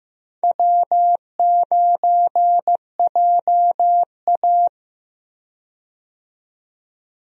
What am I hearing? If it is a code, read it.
Morse code, heard as a keyed tone, about 15 wpm, 700 Hz. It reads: W9JA
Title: USA Call Signs - 4 Characters - 15wpm